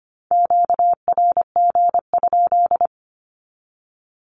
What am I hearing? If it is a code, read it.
Morse code, heard as a keyed tone, about 25 wpm, 700 Hz. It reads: QRG?
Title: Q-Codes 25wpm